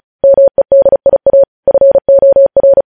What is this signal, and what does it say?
Morse code, heard as a keyed tone, about 36 wpm, 550 Hz. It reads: MEDIA FOR
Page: News Headlines in Morse Code at 35 WPM